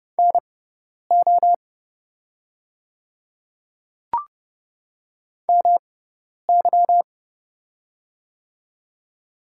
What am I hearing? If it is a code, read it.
Morse code, heard as a keyed tone, about 30 wpm, spaced as for 10 wpm, 700 Hz. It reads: NO MY
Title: 6x Character Spacing - Top 200 Words - 30wpm